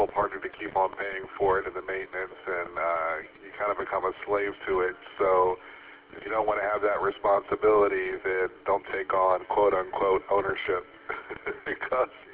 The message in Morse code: none

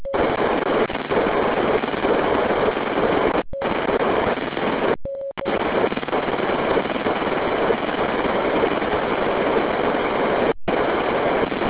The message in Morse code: none

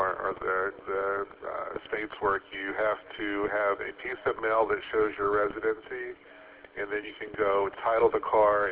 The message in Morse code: none

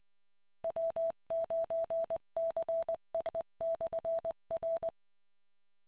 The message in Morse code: W9CEE/R

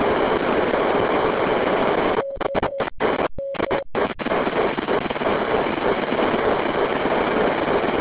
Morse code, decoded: none

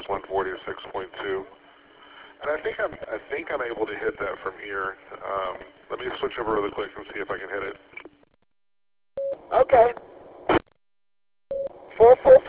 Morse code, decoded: none

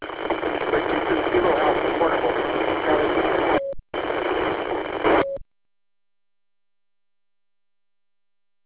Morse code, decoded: none

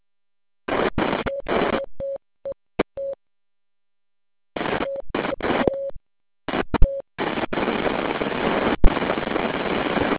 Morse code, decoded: none

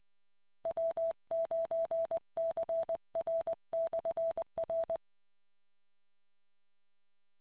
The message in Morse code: W9CR/R

